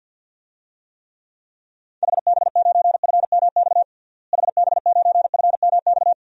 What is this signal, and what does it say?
Morse code, heard as a keyed tone, about 51 wpm, 700 Hz. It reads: HB9FMX HB9FMX
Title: DXCC-100 Call Signs - 6 Characters - Repeated 2x - 50wpm